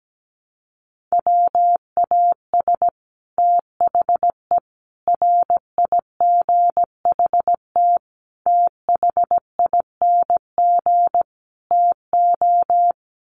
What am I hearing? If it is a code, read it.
Morse code, heard as a keyed tone, about 17 wpm, 700 Hz. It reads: WAS THE RIGHT THING TO
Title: Sets of 5 Words Encore2 17wpm